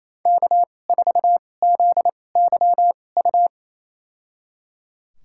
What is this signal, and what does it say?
Morse code, heard as a keyed tone, about 28 wpm, spaced as for 20 wpm, 700 Hz. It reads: K4ZYU
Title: USA Call Signs - 5 Characters - 2x Character Spacing - 28wpm